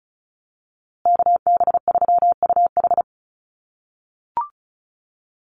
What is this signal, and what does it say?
Morse code, heard as a keyed tone, about 34 wpm, 700 Hz. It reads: KB3UH
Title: USA Call Signs Repeated 3x 35wpm